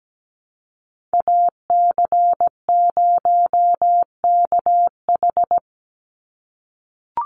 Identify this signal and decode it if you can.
Morse code, heard as a keyed tone, about 17 wpm, 700 Hz. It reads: AC0KH
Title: USA Call Signs 17wpm